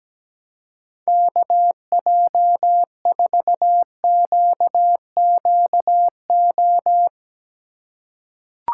Morse code, heard as a keyed tone, about 17 wpm, 700 Hz. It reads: KJ4QQO
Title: USA Call Signs - 3x Character Spacing - 17wpm